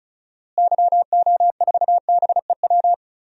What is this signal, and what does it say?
Morse code, heard as a keyed tone, about 34 wpm, 700 Hz. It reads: YO4BEW
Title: DXCC-100 Call Signs - 6 Characters - Repeated 3x - 35wpm